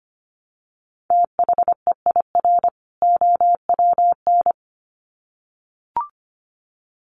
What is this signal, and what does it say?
Morse code, heard as a keyed tone, about 25 wpm, 700 Hz. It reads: THEIR OWN
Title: Sets of 2 Words 25wpm